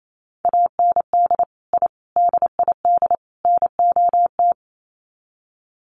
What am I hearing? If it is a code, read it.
Morse code, heard as a keyed tone, about 28 wpm, 700 Hz. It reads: AND I DID NOT
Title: Sets of 4 Words 28wpm